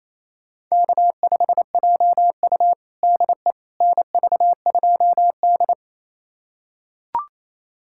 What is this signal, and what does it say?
Morse code, heard as a keyed tone, about 28 wpm, 700 Hz. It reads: K5JU DE NV2D
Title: USA Callsign DE Callsign - 4 Characters - 28wpm